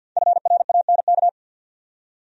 Morse code, heard as a keyed tone, about 52 wpm, 700 Hz. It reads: FRANK
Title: USA Common Names - Max 5 Letters - 50wpm